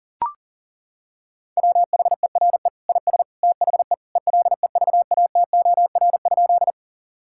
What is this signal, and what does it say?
Morse code, heard as a keyed tone, about 40 wpm, 700 Hz. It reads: WHERE IS THE ELEVATOR?
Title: Sentences from Top 1600 Words 40wpm